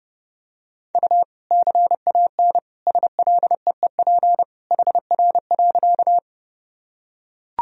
Morse code, heard as a keyed tone, about 30 wpm, 700 Hz. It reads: U CAN SLEEP HR.
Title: Sentences from Top 500 Words and Abbreviations 30wpm